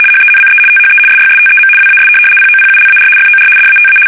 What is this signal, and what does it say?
RTTY, about 75 baud. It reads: HENCE THE SYSTEM WAS POPULAR FOR DATA
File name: fsk850-75B / teletype-corp